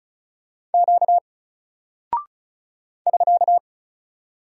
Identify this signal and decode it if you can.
Morse code, heard as a keyed tone, about 36 wpm, 700 Hz. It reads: Q <SK>
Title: Single Character (Letter:Number:Punctuation:Prosign) Alt 35wpm